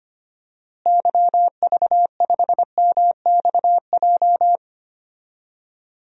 Morse code, heard as a keyed tone, about 25 wpm, 700 Hz. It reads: YV5MXJ
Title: DXCC-100 Call Signs - 6 Characters - Repeated 2x - 25wpm